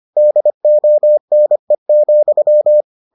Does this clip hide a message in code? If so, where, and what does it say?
Yes: Morse code from 0.2 to 2.8 s, heard as a keyed tone, about 25 wpm, 600 Hz. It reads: DONE,